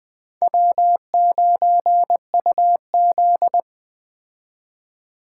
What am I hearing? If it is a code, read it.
Morse code, heard as a keyed tone, about 22 wpm, 700 Hz. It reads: W9UZ